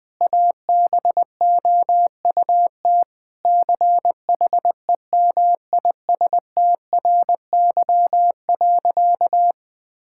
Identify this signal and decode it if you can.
Morse code, heard as a keyed tone, about 20 wpm, 700 Hz. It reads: ABOUT CHEMISTRY.
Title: Sentences from Top 1000 Words - Review Part 6 - 20wpm